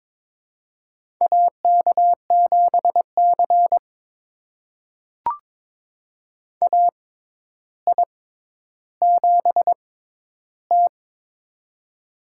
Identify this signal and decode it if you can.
Morse code, heard as a keyed tone, about 22 wpm, spaced as for 11 wpm, 700 Hz. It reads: AK7C AI7T
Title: USA Call Signs - 4 Characters - 6x Character Spacing - 22wpm